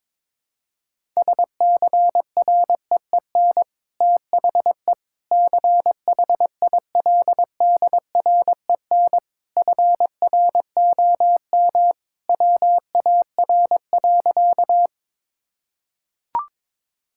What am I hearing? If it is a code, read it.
Morse code, heard as a keyed tone, about 21 wpm, 700 Hz. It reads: SCREEN THE CHILDREN FROM WAR.
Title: Sentences from Top 1000 Words - Review Part 6 - 22wpm